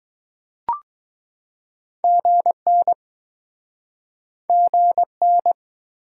Morse code, heard as a keyed tone, about 22 wpm, 700 Hz. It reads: GN GN